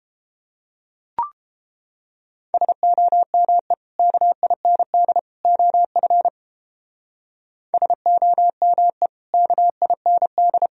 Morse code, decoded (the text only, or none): SOME KIND OF SOME KIND